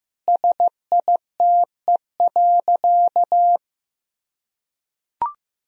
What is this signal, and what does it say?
Morse code, heard as a keyed tone, about 16 wpm, 700 Hz. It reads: SITE.
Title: Sentences from Top 1000 Words - Review Part 4 - 15wpm